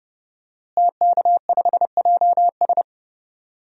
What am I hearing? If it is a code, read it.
Morse code, heard as a keyed tone, about 32 wpm, 700 Hz. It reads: TK5JS
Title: DXCC-100 Call Signs - 5 Characters - Repeated 3x - 30wpm